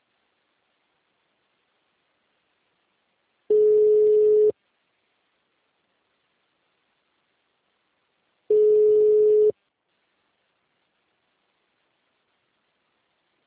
Ringback tone (Europe, 425 Hz).